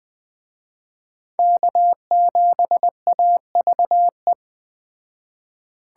Morse code, heard as a keyed tone, about 20 wpm, 700 Hz. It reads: K7AVE
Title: Speed Racing - USA 1x3 Call Signs 17wpm